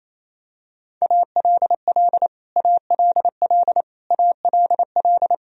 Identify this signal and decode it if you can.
Morse code, heard as a keyed tone, about 28 wpm, 700 Hz. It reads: ALL ALL ALL
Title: Top 100 Words Repeated 28wpm